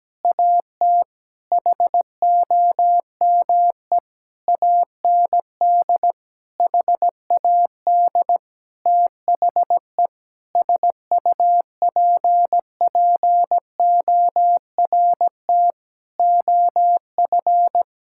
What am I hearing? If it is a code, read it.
Morse code, heard as a keyed tone, about 17 wpm, 700 Hz. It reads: AT HOME AND HAD THE SUPPORT OF
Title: Sentences from Top 1000 Words - Review Part 5 - 17wpm